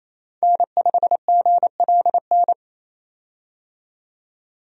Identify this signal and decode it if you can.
Morse code, heard as a keyed tone, about 28 wpm, 700 Hz. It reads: N5GLN
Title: USA Call Signs - 5 Characters - 28wpm